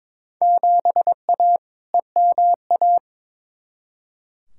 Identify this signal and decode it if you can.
Morse code, heard as a keyed tone, about 22 wpm, 700 Hz. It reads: 7A EMA
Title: Field Day Exchanges Encore - 22wpm